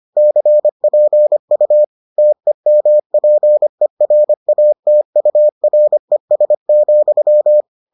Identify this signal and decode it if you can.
Morse code, heard as a keyed tone, about 25 wpm, 600 Hz. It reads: CPU TEMPERATURES,